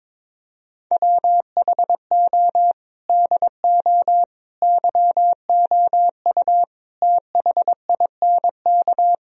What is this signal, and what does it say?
Morse code, heard as a keyed tone, about 22 wpm, 700 Hz. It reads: WHO DO YOU THINK
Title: Sentences from Top 1000 Words - Review Part 4 - 22wpm